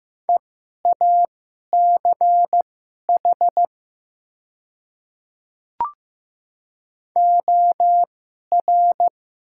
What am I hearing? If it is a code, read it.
Morse code, heard as a keyed tone, about 15 wpm, spaced as for 11 wpm, 700 Hz. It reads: EACH OR
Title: 2x Character Spacing - Top 100 Words - 15wpm